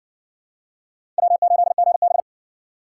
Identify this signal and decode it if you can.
Morse code, heard as a keyed tone, about 60 wpm, 700 Hz. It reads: W8GB